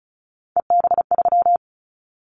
Morse code, heard as a keyed tone, about 34 wpm, 700 Hz. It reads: EB3
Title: Callsign Prefixes for Top 25 DXCC - Repeated 3x - 35wpm